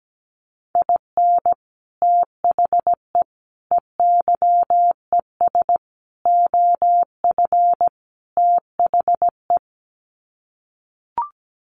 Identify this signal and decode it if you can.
Morse code, heard as a keyed tone, about 17 wpm, 700 Hz. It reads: IN THE EYES OF THE
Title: Sets of 5 Words 17wpm